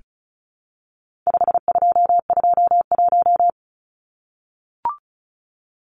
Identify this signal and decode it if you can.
Morse code, heard as a keyed tone, about 34 wpm, 700 Hz. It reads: 5221
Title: Numbers - 4-Digit 35wpm